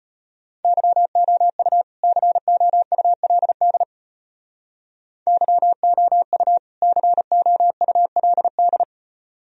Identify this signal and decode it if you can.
Morse code, heard as a keyed tone, about 30 wpm, 700 Hz. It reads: YOU COULD YOU COULD
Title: Speed-Racing - Sets of 2 Words 25wpm